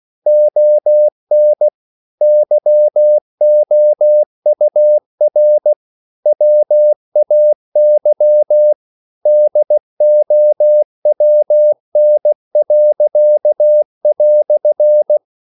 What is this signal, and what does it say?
Morse code, heard as a keyed tone, about 16 wpm, 600 Hz. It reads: ON YOUR WAY DOWN."
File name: CW / Practice / EnglishQuotations0216